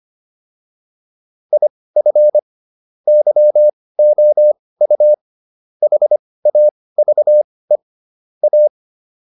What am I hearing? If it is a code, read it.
Morse code, heard as a keyed tone, about 25 wpm, spaced as for 18 wpm, 600 Hz. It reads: IF YOU HAVE A